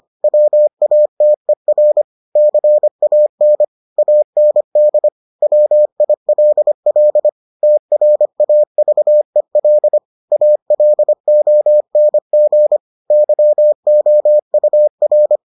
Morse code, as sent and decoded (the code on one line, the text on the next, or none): .-- .- - . .-. / -.-. .- -. / .- -. -.. / .-- .. .-.. .-.. / - .-. .- ...- . .-.. / .- .-.. --- -. --. / -.-- --- ..- .-.
WATER CAN AND WILL TRAVEL ALONG YOUR